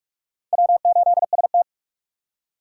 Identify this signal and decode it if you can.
Morse code, heard as a keyed tone, about 41 wpm, 700 Hz. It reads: W8ST